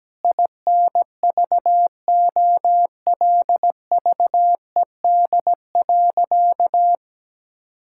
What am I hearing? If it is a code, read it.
Morse code, heard as a keyed tone, about 17 wpm, 700 Hz. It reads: INVOLVED.
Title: Sentences from Top 1000 Words - Review Part 5 - 17wpm